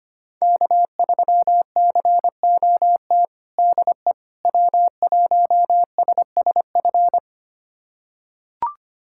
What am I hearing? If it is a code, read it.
Morse code, heard as a keyed tone, about 25 wpm, 700 Hz. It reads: K3COT DE W1SSF
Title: USA Callsign DE Callsign 25wpm